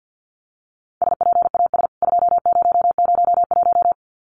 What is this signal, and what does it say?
Morse code, heard as a keyed tone, about 50 wpm, 700 Hz. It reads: SPAS 2001
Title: CWT Exchange 50wpm